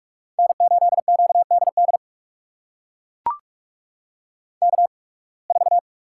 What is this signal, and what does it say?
Morse code, heard as a keyed tone, about 41 wpm, spaced as for 28 wpm, 700 Hz. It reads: N8QDD K4
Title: USA Call Signs - 5 Characters - 8x Character Spacing - 45wpm